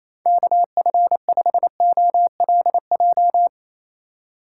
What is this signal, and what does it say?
Morse code, heard as a keyed tone, about 28 wpm, 700 Hz. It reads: KF5OLJ
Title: USA Call Signs - 6 Characters - 3x Character Spacing - 28wpm